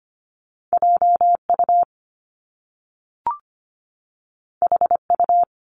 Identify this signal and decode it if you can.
Morse code, heard as a keyed tone, about 25 wpm, 700 Hz. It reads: JU HU
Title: Top Two Letter Combinations in Words 25wpm